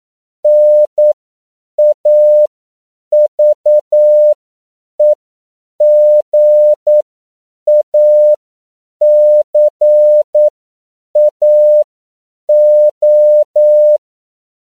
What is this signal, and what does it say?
Morse code, heard as a keyed tone, about 9 wpm, spaced as for 7 wpm, 600 Hz. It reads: NAVEGACAO
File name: CW / 10PPM